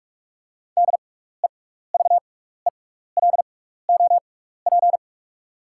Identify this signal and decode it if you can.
Morse code, heard as a keyed tone, about 47 wpm, spaced as for 16 wpm, 690 Hz. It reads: DEVELOP
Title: 6x Character Spacing - Top 500 Words - 45wpm